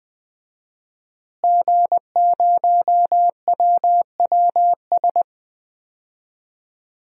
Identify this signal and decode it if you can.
Morse code, heard as a keyed tone, about 20 wpm, 700 Hz. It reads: G0WWS
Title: DXCC-100 Call Signs - 5 Characters - Repeated 2x - 20wpm